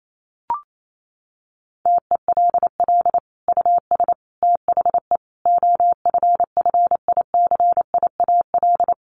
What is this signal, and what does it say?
Morse code, heard as a keyed tone, about 28 wpm, 700 Hz. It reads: TELL US THE OFFICIAL
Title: Sentences from Top 1400 Words 28wpm